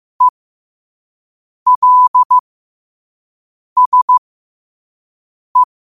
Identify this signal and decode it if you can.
Morse code, heard as a keyed tone, about 15 wpm, spaced as for 5 wpm, 1000 Hz. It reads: ELSE
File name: qotdfiles / 5wpm